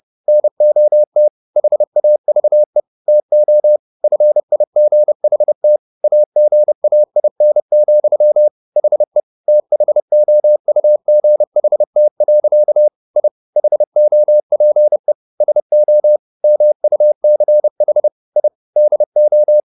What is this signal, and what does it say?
Morse code, heard as a keyed tone, about 30 wpm, 600 Hz. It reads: NOT HAVE TO FIGHT AGAIN, HE THOUGHT. I HOPE SO MUCH I DO